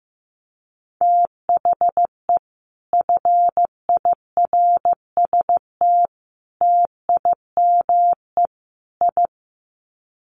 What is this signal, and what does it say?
Morse code, heard as a keyed tone, about 15 wpm, 700 Hz. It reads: THE FIRST TIME I
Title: Sets of 4 Words 15wpm